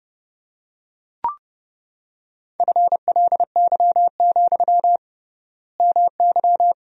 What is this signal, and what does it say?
Morse code, heard as a keyed tone, about 30 wpm, 700 Hz. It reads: FLY, MY
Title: 3x Word-Spacing - Sentences from Top 500 Words - 30wpm